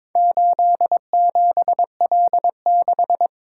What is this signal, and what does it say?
Morse code, heard as a keyed tone, about 22 wpm, 700 Hz. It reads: 87L6
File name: practice / 22wpm / Four Letter-Number Combinations - 22wpm